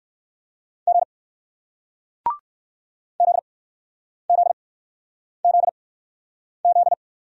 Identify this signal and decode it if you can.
Morse code, heard as a keyed tone, about 60 wpm, spaced as for 9 wpm, 700 Hz. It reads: G ZZZZ